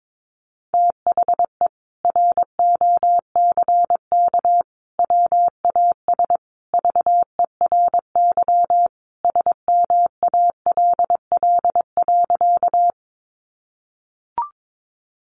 Morse code, heard as a keyed tone, about 22 wpm, 700 Hz. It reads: THE ROCK WAS VERY SMALL.
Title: Sentences from Top 400 Words 22wpm